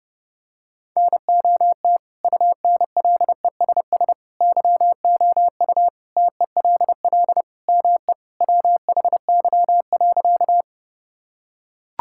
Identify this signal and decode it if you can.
Morse code, heard as a keyed tone, about 32 wpm, 700 Hz. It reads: NOT UNLESS YOU TELL ME WHY.
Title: Sentences from Top 1000 Words - Review Part 6 - 30wpm